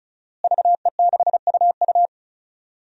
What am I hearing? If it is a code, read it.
Morse code, heard as a keyed tone, about 34 wpm, 700 Hz. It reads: VE6UU
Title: DXCC-100 Call Signs - 5 Characters - Repeated 2x - 35wpm